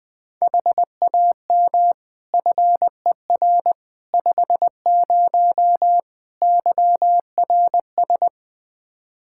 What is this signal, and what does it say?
Morse code, heard as a keyed tone, about 21 wpm, 700 Hz. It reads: HAM FER 50 YRS
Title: Practice QSO - Exchange 3 - More Info - 20wpm